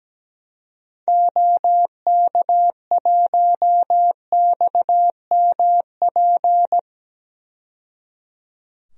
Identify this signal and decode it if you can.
Morse code, heard as a keyed tone, about 17 wpm, 700 Hz. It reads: OK1XMP
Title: DXCC-100 Call Signs - 6 Characters - 17wpm